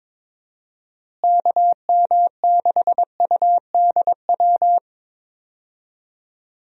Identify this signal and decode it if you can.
Morse code, heard as a keyed tone, about 22 wpm, 700 Hz. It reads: KM6UDW
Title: USA Call Signs - 6 Characters - Repeated 3x - 22wpm